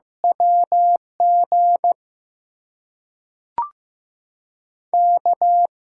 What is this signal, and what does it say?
Morse code, heard as a keyed tone, about 15 wpm, 700 Hz. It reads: WG K
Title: USA Call Signs - 6 Characters - 8x Character Spacing - 15wpm